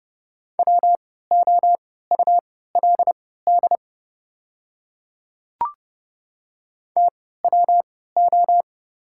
Morse code, heard as a keyed tone, about 30 wpm, spaced as for 17 wpm, 700 Hz. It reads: WOULD TWO
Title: 3x Character Spacing - Top 100 Words - 30wpm